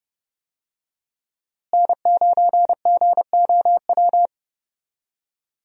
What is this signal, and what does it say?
Morse code, heard as a keyed tone, about 32 wpm, 700 Hz. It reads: N9GOW
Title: USA Call Signs - 5 Characters - Repeated 2x - 30wpm